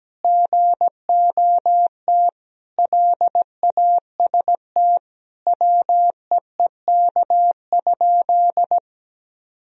Morse code, heard as a keyed tone, about 17 wpm, 700 Hz. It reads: GOT LAST WEEK?
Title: Sentences from Top 1000 Words - Review Part 3 - 17wpm